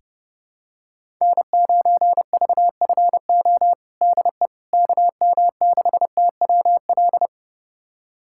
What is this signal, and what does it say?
Morse code, heard as a keyed tone, about 30 wpm, 700 Hz. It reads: N9VFO DE KM6TWL
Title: USA Callsign DE Callsign 30wpm